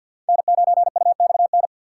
Morse code, heard as a keyed tone, about 53 wpm, 700 Hz. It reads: N9UXN
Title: USA Call Signs - 5 Characters - 4x Character Spacing - 50wpm